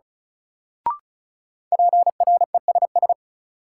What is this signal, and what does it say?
Morse code, heard as a keyed tone, about 34 wpm, 700 Hz. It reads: PRESS
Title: Sentences from Top 800 Words - New Word Drill - 35wpm